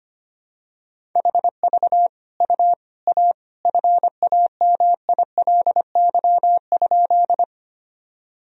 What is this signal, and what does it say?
Morse code, heard as a keyed tone, about 25 wpm, 700 Hz. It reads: HV U A FAMILY?
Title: Sentences from Top 400 Words and Abbreviations 25wpm